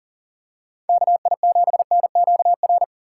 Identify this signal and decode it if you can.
Morse code, heard as a keyed tone, about 40 wpm, 700 Hz. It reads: KI7NQR
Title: USA Call Signs - 6 Characters - 4x Character Spacing - 40wpm